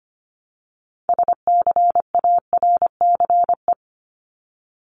Morse code, heard as a keyed tone, about 25 wpm, 700 Hz. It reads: SCARCE
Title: LY Suffix - Top 100 Words with Base Words 25wpm